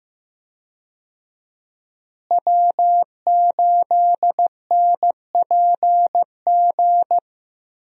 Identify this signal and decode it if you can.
Morse code, heard as a keyed tone, about 15 wpm, 700 Hz. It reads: W8NPG
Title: USA Call Signs - 5 Characters - Repeated 3x - 15wpm